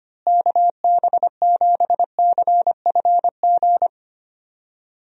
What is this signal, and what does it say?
Morse code, heard as a keyed tone, about 25 wpm, 700 Hz. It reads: KB7CFG
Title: USA Call Signs - 6 Characters - Repeated 3x - 25wpm